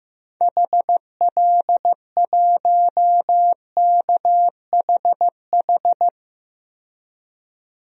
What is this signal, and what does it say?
Morse code, heard as a keyed tone, about 15 wpm, 700 Hz. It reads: HL1KHH
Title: DXCC-100 Call Signs - 6 Characters - 15wpm